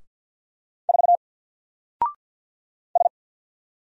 Morse code, heard as a keyed tone, about 48 wpm, 700 Hz. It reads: 4 S
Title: Single Character (Letter:Number:Punctuation:Prosign) Alt 50wpm